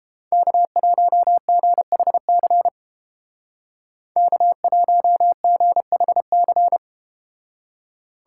Morse code, heard as a keyed tone, about 32 wpm, 700 Hz. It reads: K1GHC K1GHC